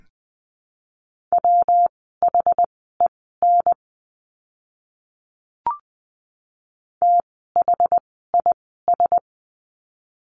Morse code, heard as a keyed tone, about 20 wpm, spaced as for 15 wpm, 700 Hz. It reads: WHEN THIS
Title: 160 Lesson 26f - Letter g - Words 20wpm